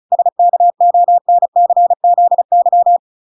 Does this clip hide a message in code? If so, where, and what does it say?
Yes: Morse code from 0.1 to 3.0 s, heard as a keyed tone, about 34 wpm, 700 Hz. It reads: SKONCZY